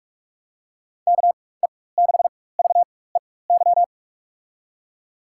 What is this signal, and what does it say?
Morse code, heard as a keyed tone, about 45 wpm, spaced as for 21 wpm, 700 Hz. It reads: KE6VEY
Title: USA Call Signs - 6 Characters - 4x Character Spacing - 45wpm